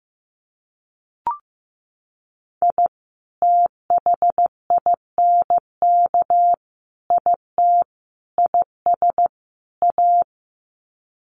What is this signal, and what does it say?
Morse code, heard as a keyed tone, about 14 wpm, 700 Hz. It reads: I THINK IT IS A
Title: Sets of 5 Words Encore 15wpm